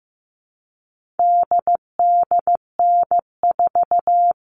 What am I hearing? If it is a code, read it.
Morse code, heard as a keyed tone, about 15 wpm, 700 Hz. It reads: DDN4
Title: All Cut Numbers - 4-Digit 15wpm